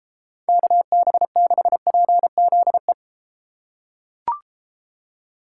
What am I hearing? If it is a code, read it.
Morse code, heard as a keyed tone, about 32 wpm, 700 Hz. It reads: KB6PZE